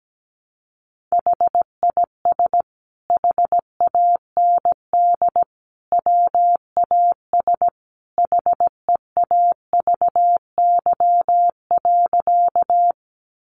Morse code, heard as a keyed tone, about 17 wpm, 700 Hz. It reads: HIS HAND WAS HEAVY.
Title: Sentences from Top 1100 Words 17wpm